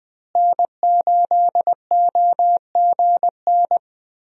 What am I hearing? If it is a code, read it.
Morse code, heard as a keyed tone, about 20 wpm, 700 Hz. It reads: N8OGN
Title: USA Call Signs - 5 Characters - Repeated 2x - 20wpm